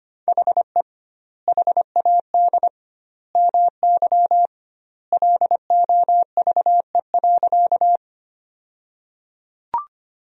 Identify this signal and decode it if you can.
Morse code, heard as a keyed tone, about 25 wpm, 700 Hz. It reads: HE HAD MY LOVE.
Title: 2x Word-Spacing - Sentences from Top 400 Words - 25wpm